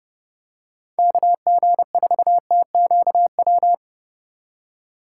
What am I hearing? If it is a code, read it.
Morse code, heard as a keyed tone, about 30 wpm, 700 Hz. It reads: KG4TQW